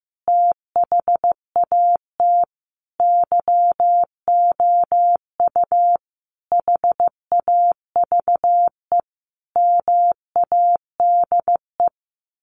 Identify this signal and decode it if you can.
Morse code, heard as a keyed tone, about 15 wpm, 700 Hz. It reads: THAT YOU HAVE MADE
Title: Sentences from Top 1400 Words 15wpm